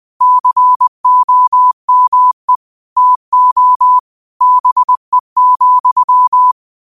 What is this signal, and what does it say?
Morse code, heard as a keyed tone, about 20 wpm, 1000 Hz. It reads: COME TO BE,
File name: qotdfiles / 20wpm